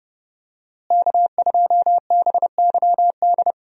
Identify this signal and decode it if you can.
Morse code, heard as a keyed tone, about 30 wpm, 700 Hz. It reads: K2BYD
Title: USA Call Signs - 5 Characters - Repeated 2x - 30wpm